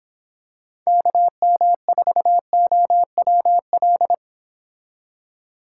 Morse code, heard as a keyed tone, about 26 wpm, 700 Hz. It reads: KM4OWL